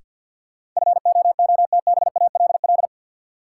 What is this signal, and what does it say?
Morse code, heard as a keyed tone, about 53 wpm, 700 Hz. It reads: FOOTBALL
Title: Sentences from Top 1600 Words - New Word Drill Alt - 50wpm